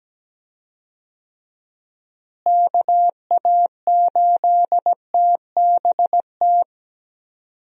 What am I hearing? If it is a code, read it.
Morse code, heard as a keyed tone, about 17 wpm, 700 Hz. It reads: KA8TBT
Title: USA Call Signs - 6 Characters - Repeated 2x - 17wpm